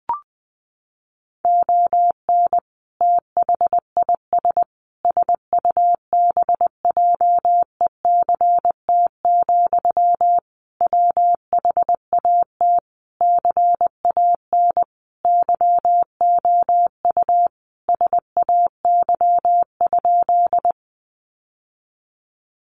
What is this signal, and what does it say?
Morse code, heard as a keyed tone, about 20 wpm, 700 Hz. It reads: ON THIS SUBJECT, WHAT CAN YOU SAY?
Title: Sentences from Top 800 Words 20wpm